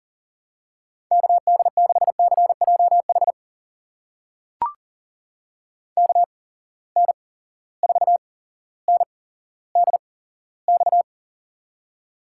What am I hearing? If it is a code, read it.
Morse code, heard as a keyed tone, about 39 wpm, spaced as for 17 wpm, 700 Hz. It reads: KD6CJH KN4NDX
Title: USA Call Signs - 8x Character Spacing - 40wpm